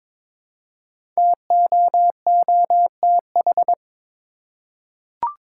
Morse code, heard as a keyed tone, about 22 wpm, 690 Hz. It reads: TOOTH
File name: 5 Letter Words - First 3 Letters Given - 22wpm